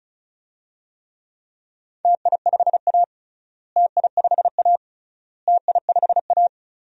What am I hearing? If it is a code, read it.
Morse code, heard as a keyed tone, about 36 wpm, 700 Hz. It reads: TI5A TI5A TI5A
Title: DXCC-100 Call Signs - 4 Characters - Repeated 3x - 35wpm